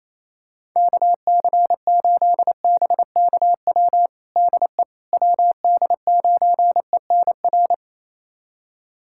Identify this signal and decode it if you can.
Morse code, heard as a keyed tone, about 28 wpm, 700 Hz. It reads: KC8BKW DE WD9ENR
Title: USA Callsign DE Callsign - 6 Characters - 28wpm